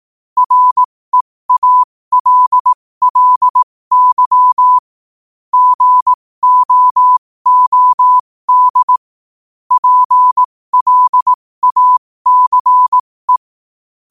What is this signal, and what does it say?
Morse code, heard as a keyed tone, about 18 wpm, spaced as for 15 wpm, 1000 Hz. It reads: REALLY GOOD PLACE
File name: qotdfiles / 15wpm